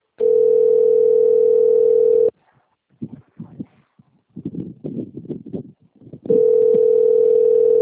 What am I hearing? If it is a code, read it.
Ringback tone (North America), 440 + 480 Hz.